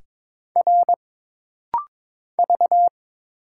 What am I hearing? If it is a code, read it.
Morse code, heard as a keyed tone, about 22 wpm, 700 Hz. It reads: R V
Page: This rapid-fire timing is geared to push your ICR (Instant Character Recognition) skills.